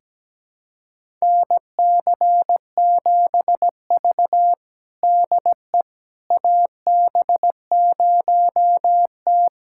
Morse code, heard as a keyed tone, about 17 wpm, 700 Hz. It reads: NC7V DE AB0T
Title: USA Callsign DE Callsign - 4 Characters - 17wpm